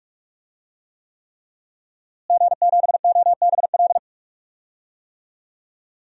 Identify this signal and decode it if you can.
Morse code, heard as a keyed tone, about 46 wpm, 700 Hz. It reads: G7OBL
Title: DXCC-100 Call Signs - 5 Characters - Repeated 3x - 45wpm